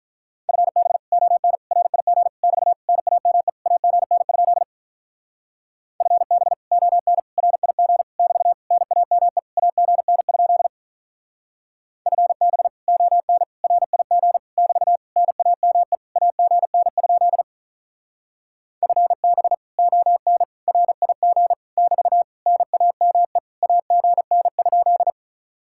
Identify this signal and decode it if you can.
Morse code, heard as a keyed tone, about 44 wpm, 700 Hz. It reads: FB ON RIG <BT> NAME AGN? FB ON RIG <BT> NAME AGN? FB ON RIG <BT> NAME AGN? FB ON RIG <BT> NAME AGN?